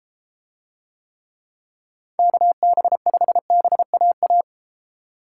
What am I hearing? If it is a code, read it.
Morse code, heard as a keyed tone, about 33 wpm, 700 Hz. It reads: KB5BAA